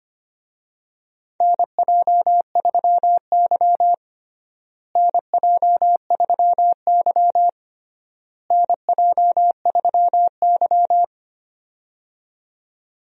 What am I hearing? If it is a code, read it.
Morse code, heard as a keyed tone, about 25 wpm, 700 Hz. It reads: NJ3Y NJ3Y NJ3Y
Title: USA Call Signs - 4 Characters - Repeated 3x - 25wpm